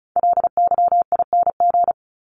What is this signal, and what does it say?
Morse code, heard as a keyed tone, about 34 wpm, 700 Hz. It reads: LYING
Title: ING Suffix - Top 100 Words with Base Words 35wpm